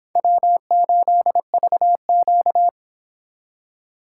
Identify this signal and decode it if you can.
Morse code, heard as a keyed tone, about 25 wpm, 700 Hz. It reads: W8VQ